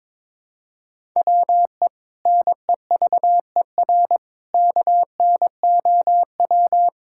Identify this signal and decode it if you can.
Morse code, heard as a keyed tone, about 22 wpm, 700 Hz. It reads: WE NEVER KNOW
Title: Sentences from Top 1600 Words 22wpm